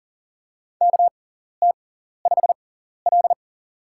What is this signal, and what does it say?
Morse code, heard as a keyed tone, about 40 wpm, spaced as for 14 wpm, 700 Hz. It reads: KT5L
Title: USA Call Signs - 4 Characters - 6x Character Spacing - 40wpm